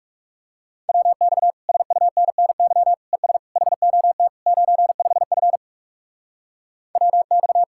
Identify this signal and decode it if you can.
Morse code, heard as a keyed tone, about 38 wpm, 700 Hz. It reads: WX SUNNY ES HOT 95F WX